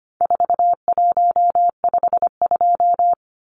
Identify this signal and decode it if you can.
Morse code, heard as a keyed tone, about 25 wpm, 700 Hz. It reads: 4152
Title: Numbers - 4-Digit 25wpm